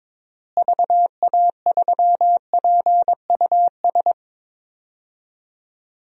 Morse code, heard as a keyed tone, about 22 wpm, 700 Hz. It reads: VA3PUS
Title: DXCC-100 Call Signs - 6 Characters - Repeated 2x - 22wpm